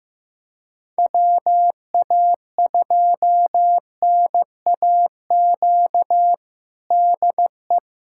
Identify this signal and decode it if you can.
Morse code, heard as a keyed tone, about 15 wpm, 700 Hz. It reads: WA2NAQ DE
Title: USA Callsign DE Callsign - 6 Characters - 15wpm